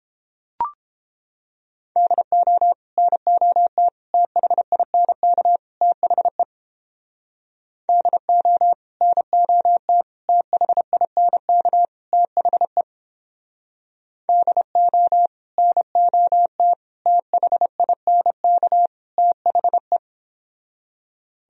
Morse code, heard as a keyed tone, about 30 wpm, 700 Hz. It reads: DO NOT THINK THE DO NOT THINK THE DO NOT THINK THE